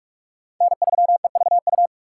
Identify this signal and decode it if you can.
Morse code, heard as a keyed tone, about 47 wpm, 690 Hz. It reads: N3EVU
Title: USA Call Signs - 5 Characters - 6x Character Spacing - 45wpm